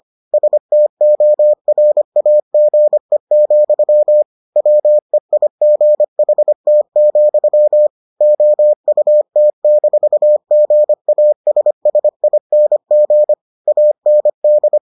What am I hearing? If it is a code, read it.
Morse code, heard as a keyed tone, about 25 wpm, 600 Hz. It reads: STORAGE, WEIGHT, OUT-GASSING AND